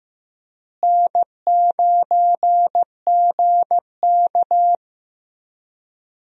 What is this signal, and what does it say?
Morse code, heard as a keyed tone, about 15 wpm, 700 Hz. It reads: N9GK
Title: USA Call Signs - 4 Characters - Repeated 3x - 15wpm